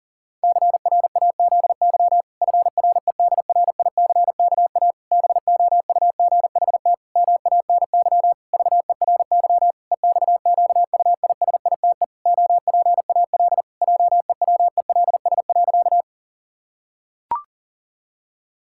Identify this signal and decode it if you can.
Morse code, heard as a keyed tone, about 40 wpm, 700 Hz. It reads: CRAZY FREDRICKA BOUGHT MANY VERY EXQUISITE OPAL JEWELS.
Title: Pangram Challenge v2 - 40wpm